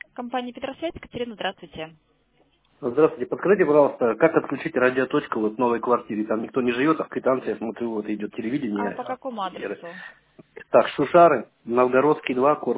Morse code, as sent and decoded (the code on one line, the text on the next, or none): none